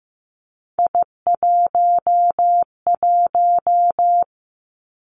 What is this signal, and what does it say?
Morse code, heard as a keyed tone, about 15 wpm, 700 Hz. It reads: I11
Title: Three Challenging Letter-Number Combinations 15wpm